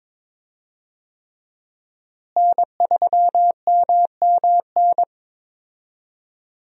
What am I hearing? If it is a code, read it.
Morse code, heard as a keyed tone, about 22 wpm, 700 Hz. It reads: N3MMN
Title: USA Call Signs - 5 Characters - Repeated 3x - 22wpm